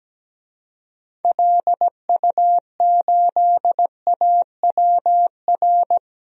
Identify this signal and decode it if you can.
Morse code, heard as a keyed tone, about 17 wpm, 700 Hz. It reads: LU8AWR
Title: DXCC-100 Call Signs - 6 Characters - 17wpm